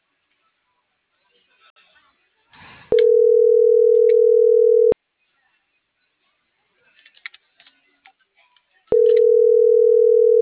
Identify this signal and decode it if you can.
Ringback tone (North America), 440 + 480 Hz.